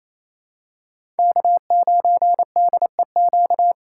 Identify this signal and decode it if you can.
Morse code, heard as a keyed tone, about 28 wpm, 700 Hz. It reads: K9DEQ
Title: USA Call Signs - 5 Characters - 4x Character Spacing - 28wpm